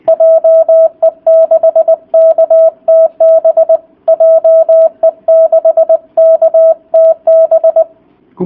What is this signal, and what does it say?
Morse code, heard as a keyed tone, about 20 wpm, 650 Hz. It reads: JE6KTB JE6KTB